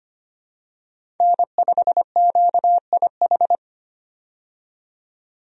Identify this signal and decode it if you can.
Morse code, heard as a keyed tone, about 25 wpm, 700 Hz. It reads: N5QIH
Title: USA Call Signs - 5 Characters - 25wpm